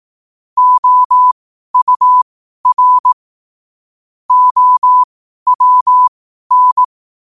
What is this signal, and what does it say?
Morse code, heard as a keyed tone, about 18 wpm, spaced as for 12 wpm, 1000 Hz. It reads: OUR OWN
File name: qotdfiles / 12wpm